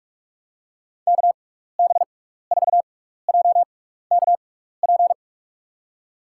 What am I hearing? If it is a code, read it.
Morse code, heard as a keyed tone, about 46 wpm, spaced as for 15 wpm, 700 Hz. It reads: KB4JKP
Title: USA Call Signs - 6 Characters - 6x Character Spacing - 45wpm